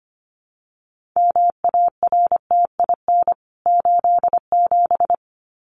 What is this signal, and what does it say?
Morse code, heard as a keyed tone, about 25 wpm, 700 Hz. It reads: MARTIN 87
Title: CWT Exchange 25wpm